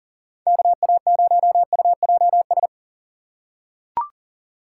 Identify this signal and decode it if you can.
Morse code, heard as a keyed tone, about 40 wpm, 700 Hz. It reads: KA0UJS
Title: USA Call Signs - 6 Characters - 4x Character Spacing - 40wpm